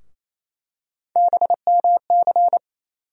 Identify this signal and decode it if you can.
Morse code, heard as a keyed tone, about 28 wpm, 700 Hz. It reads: BMC
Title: USA Call Sign Postfixes - 3 Letters 28wpm